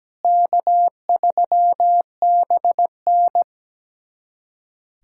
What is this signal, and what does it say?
Morse code, heard as a keyed tone, about 17 wpm, 700 Hz. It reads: K3BN
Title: USA Callsign DE Callsign - 4 Characters - 17wpm